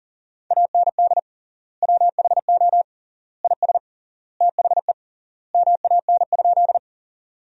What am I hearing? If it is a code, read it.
Morse code, heard as a keyed tone, about 40 wpm, 700 Hz. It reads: AND WHO IS THE MAN?
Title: 3x Word-Spacing - Sentences from Top 200 Words - 40wpm